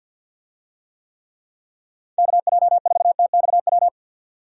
Morse code, heard as a keyed tone, about 49 wpm, 700 Hz. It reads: KJ4TXW
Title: USA Call Signs - 6 Characters - 50wpm